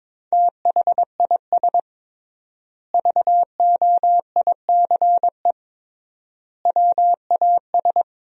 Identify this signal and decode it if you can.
Morse code, heard as a keyed tone, about 22 wpm, 700 Hz. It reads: THIS VOICE WAS
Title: 3x Word-Spacing - Sentences from Top 500 Words - 22wpm